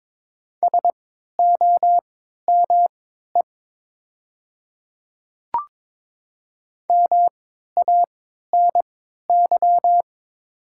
Morse code, heard as a keyed tone, about 22 wpm, spaced as for 12 wpm, 700 Hz. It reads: SOME MANY
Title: 3x Character Spacing - Top 200 Words - 22wpm